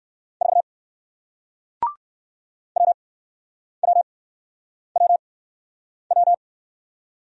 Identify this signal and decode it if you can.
Morse code, heard as a keyed tone, about 60 wpm, spaced as for 9 wpm, 700 Hz. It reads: 4 WWWW